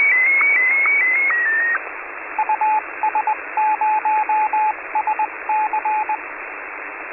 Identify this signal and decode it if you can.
Morse code, heard as a keyed tone, about 20 wpm, 860 Hz, with radio noise. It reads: US0SC